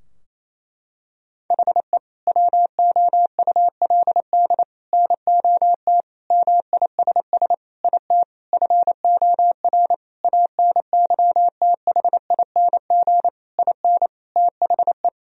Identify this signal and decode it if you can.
Morse code, heard as a keyed tone, about 27 wpm, 700 Hz. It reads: HE WOULD NOT MISS IT FOR ANYTHING IN THE
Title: Sentences from Top 1000 Words - Review Part 3 - 28wpm